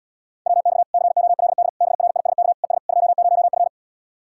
Morse code, heard as a keyed tone, about 79 wpm, 700 Hz. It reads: WX COLD BRISK ES 20F